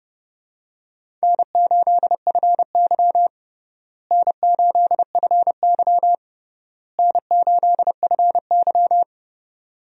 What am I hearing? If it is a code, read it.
Morse code, heard as a keyed tone, about 30 wpm, 700 Hz. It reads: N8FY N8FY N8FY
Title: USA Call Signs - 4 Characters - Repeated 3x - 30wpm